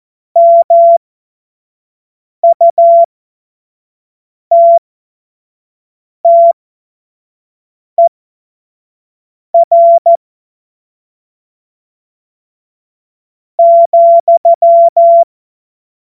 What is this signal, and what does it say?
Morse code, heard as a keyed tone, about 14 wpm, spaced as for 4 wpm, 680 Hz. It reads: MUTTER,
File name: Morsekurs / Witze / Witz 31. August 2021, 25 F70